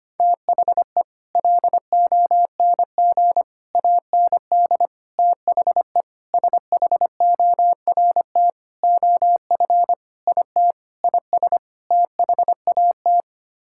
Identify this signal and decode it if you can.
Morse code, heard as a keyed tone, about 25 wpm, 700 Hz. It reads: THE LONG AND THE SHORT OF IT IS THAT
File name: Sets of 9 Words - 25wpm